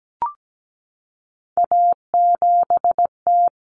Morse code, heard as a keyed tone, about 17 wpm, 690 Hz. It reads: A7T
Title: Common Cut Numbers - 3-Digit 17wpm